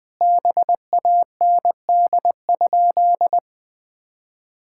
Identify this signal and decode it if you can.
Morse code, heard as a keyed tone, about 20 wpm, 700 Hz. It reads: BAND?
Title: Sentences from Top 1600 Words 20wpm